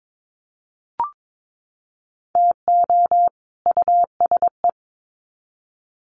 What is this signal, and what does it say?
Morse code, heard as a keyed tone, about 22 wpm, 700 Hz. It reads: TO USE
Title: Sets of 2 Words 22wpm